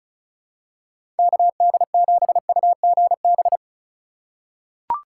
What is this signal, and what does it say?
Morse code, heard as a keyed tone, about 36 wpm, 700 Hz. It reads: KD7UGB
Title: USA Call Signs - 6 Characters - 4x Character Spacing - 35wpm